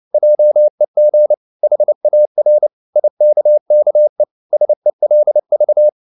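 Morse code, heard as a keyed tone, about 29 wpm, 600 Hz. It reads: JEG HAR IKKE SELV
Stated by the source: Fast (QRQ) Morse at 29 wpm, from CW med Gnister.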